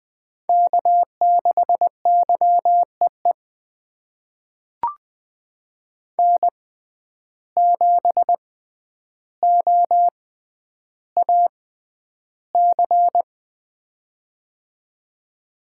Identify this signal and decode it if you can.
Morse code, heard as a keyed tone, about 21 wpm, spaced as for 10 wpm, 700 Hz. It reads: K6YEE N7OAC
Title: USA Call Signs - 5 Characters - 6x Character Spacing - 20wpm